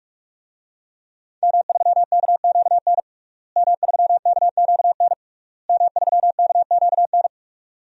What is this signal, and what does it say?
Morse code, heard as a keyed tone, about 46 wpm, 700 Hz. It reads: M3KQN M3KQN M3KQN
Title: DXCC-100 Call Signs - 5 Characters - Repeated 3x - 45wpm